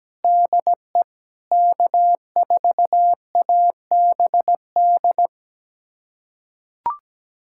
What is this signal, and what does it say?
Morse code, heard as a keyed tone, about 17 wpm, 700 Hz. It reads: DE K4ABD
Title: USA Callsign DE Callsign 17wpm